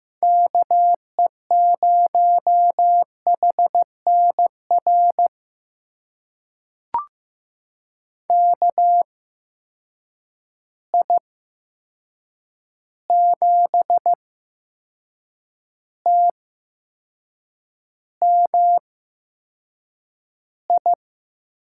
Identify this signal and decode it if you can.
Morse code, heard as a keyed tone, about 15 wpm, spaced as for 6 wpm, 700 Hz. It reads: KE0HNR KI7TMI
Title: USA Call Signs - 6 Characters - 8x Character Spacing - 15wpm